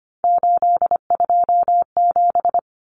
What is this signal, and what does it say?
Morse code, heard as a keyed tone, about 25 wpm, 700 Hz. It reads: 827
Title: Common Cut Numbers - 3-Digit 25wpm